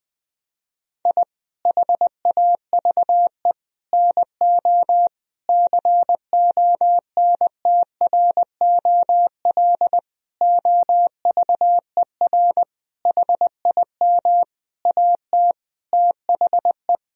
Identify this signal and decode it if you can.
Morse code, heard as a keyed tone, about 20 wpm, 700 Hz. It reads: I HAVE NO CONTROL OVER HIM AT THE
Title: Sentences from Top 1000 Words - Review Part 3 - 20wpm